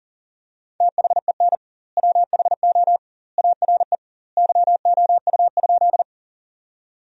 Morse code, heard as a keyed tone, about 40 wpm, 700 Hz. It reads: THEN WHO ARE YOU?
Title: 2x Word-Spacing - Sentences from Top 100 Words - 40wpm